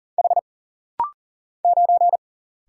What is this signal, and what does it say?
Morse code, heard as a keyed tone, about 40 wpm, 700 Hz. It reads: H 9
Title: Single Letter-Number - Rapid-Fire 40wpm